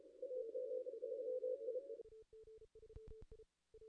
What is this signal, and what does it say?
Morse code, heard as a keyed tone, about 30 wpm, 510 Hz. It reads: AG3I
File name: contests / ss-cw-k5tr-op-n5rz -2200